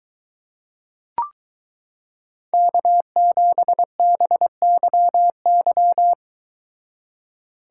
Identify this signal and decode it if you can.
Morse code, heard as a keyed tone, about 22 wpm, 700 Hz. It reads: K7BYY